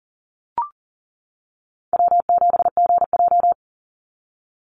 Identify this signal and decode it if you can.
Morse code, heard as a keyed tone, about 40 wpm, 700 Hz. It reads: W7GJ
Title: NA SOTA Chaser Callsigns 40wpm